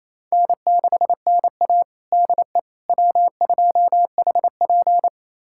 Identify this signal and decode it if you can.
Morse code, heard as a keyed tone, about 28 wpm, 700 Hz. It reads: N6NA DE W2HP
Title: USA Callsign DE Callsign - 4 Characters - 28wpm